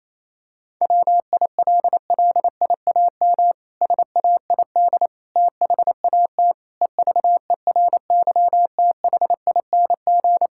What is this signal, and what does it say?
Morse code, heard as a keyed tone, about 27 wpm, 700 Hz. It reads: WILLIAM SAID THAT EVERYTHING
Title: Pangram Challenge v2 - 28wpm